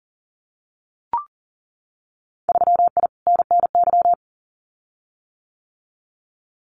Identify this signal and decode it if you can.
Morse code, heard as a keyed tone, about 42 wpm, 700 Hz. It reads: 3I NNY
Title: Winter Field Day Exchanges 40wpm